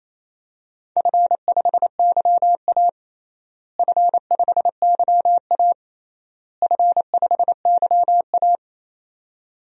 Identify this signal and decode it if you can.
Morse code, heard as a keyed tone, about 28 wpm, 700 Hz. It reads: F5YA F5YA F5YA
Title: DXCC-100 Call Signs - 4 Characters - Repeated 3x - 28wpm